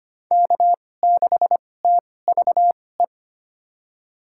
Morse code, heard as a keyed tone, about 25 wpm, spaced as for 18 wpm, 700 Hz. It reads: K6TVE
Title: USA Call Signs - 5 Characters - 2x Character Spacing - 25wpm